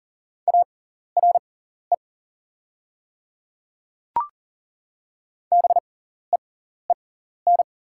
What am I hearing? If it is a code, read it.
Morse code, heard as a keyed tone, about 37 wpm, spaced as for 14 wpm, 690 Hz. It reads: ARE BEEN
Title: 6x Character Spacing - Top 100 Words - 40wpm